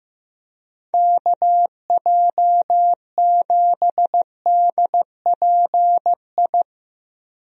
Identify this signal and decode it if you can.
Morse code, heard as a keyed tone, about 15 wpm, 700 Hz. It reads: KJ7DPI
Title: USA Call Signs - 6 Characters - Repeated 3x - 15wpm